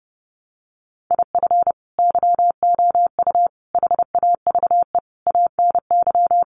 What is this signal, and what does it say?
Morse code, heard as a keyed tone, about 30 wpm, 700 Hz. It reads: IF YOU HAVE ANY
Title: Sentences from Top 1100 Words 30wpm